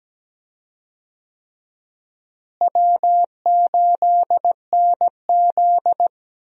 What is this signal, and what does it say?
Morse code, heard as a keyed tone, about 17 wpm, 700 Hz. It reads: W8NZ
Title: USA Call Signs - 4 Characters - Repeated 3x - 17wpm